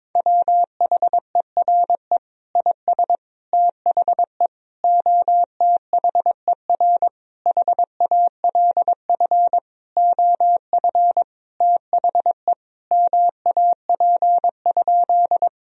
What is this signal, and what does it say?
Morse code, heard as a keyed tone, about 22 wpm, 700 Hz. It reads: WHERE IS THE OTHER HALF OF THE MAP?
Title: Sentences from Top 1000 Words - Review Part 2 - 22wpm